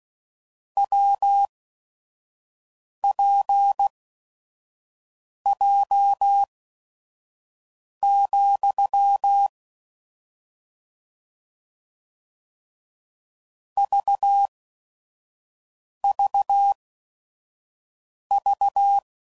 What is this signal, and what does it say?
Morse code, heard as a keyed tone, about 16 wpm, spaced as for 5 wpm, 800 Hz. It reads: WPJ, VVV
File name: Combinations / Difficult Combos / WPJ WPJ1